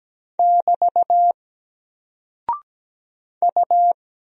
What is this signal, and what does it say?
Morse code, heard as a keyed tone, about 17 wpm, 700 Hz. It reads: <BT> U
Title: Single Character (Letter:Number:Punctuation:Prosign) 17wpm